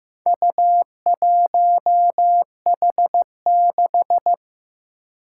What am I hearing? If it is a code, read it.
Morse code, heard as a keyed tone, about 15 wpm, 700 Hz. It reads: U1H6
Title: Four Challenging Letter-Number Combinations 15wpm